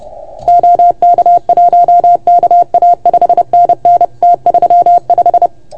Morse code, heard as a keyed tone, about 29 wpm, 690 Hz. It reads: OK1KA 5NN T35
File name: Archiv / CUC381 / OK1FGD CUC 381